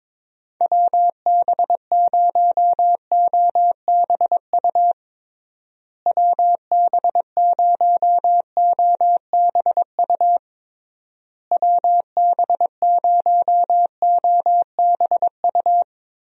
Morse code, heard as a keyed tone, about 22 wpm, 700 Hz. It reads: WB0OBU WB0OBU WB0OBU
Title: USA Call Signs - 6 Characters - Repeated 3x - 22wpm